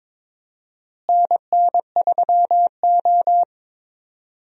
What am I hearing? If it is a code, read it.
Morse code, heard as a keyed tone, about 22 wpm, 700 Hz. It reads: NN3O
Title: USA Call Signs - 4 Characters - Repeated 3x - 22wpm